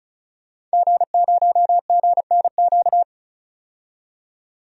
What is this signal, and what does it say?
Morse code, heard as a keyed tone, about 36 wpm, 700 Hz. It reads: G0GNQ
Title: DXCC-100 Call Signs - 5 Characters - 35wpm